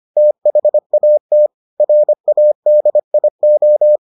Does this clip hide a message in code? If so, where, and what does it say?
Yes: Morse code from 0.2 to 4.0 s, heard as a keyed tone, about 25 wpm, 600 Hz. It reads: THAT RADIO